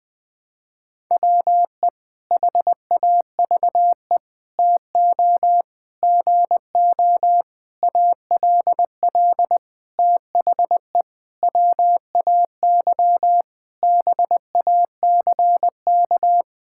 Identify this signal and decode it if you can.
Morse code, heard as a keyed tone, about 20 wpm, 700 Hz. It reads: WE HAVE TO GO ALL THE WAY BACK
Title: Sentences with 8-Word Phrases - 20wpm